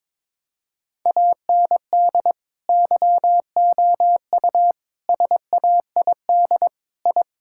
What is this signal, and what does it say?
Morse code, heard as a keyed tone, about 22 wpm, 700 Hz. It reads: AND YOU SAID I
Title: Sentences from Top 1000 Words - Review Part 5 - 22wpm